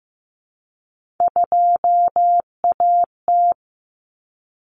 Morse code, heard as a keyed tone, about 15 wpm, 700 Hz. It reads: 2AT
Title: Common Cut Numbers - 3-Digit 15wpm